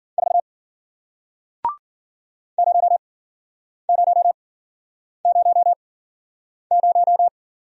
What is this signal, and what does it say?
Morse code, heard as a keyed tone, about 52 wpm, spaced as for 9 wpm, 700 Hz. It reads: 4 0000